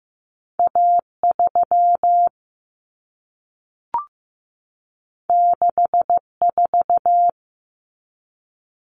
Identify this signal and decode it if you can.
Morse code, heard as a keyed tone, about 15 wpm, 700 Hz. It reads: A3 64
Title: Common Cut Numbers - 2-Digit 15wpm